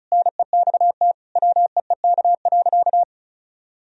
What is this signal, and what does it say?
Morse code, heard as a keyed tone, about 36 wpm, 700 Hz. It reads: NEXT WEEK.
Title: Sentences from Top 1000 Words - Review Part 4 - 35wpm